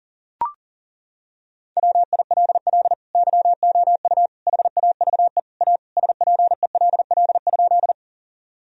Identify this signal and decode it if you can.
Morse code, heard as a keyed tone, about 37 wpm, 700 Hz. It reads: WILL YOU HAVE A SPELL?
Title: Sentences from Top 1000 Words - Review Part 1 - 40wpm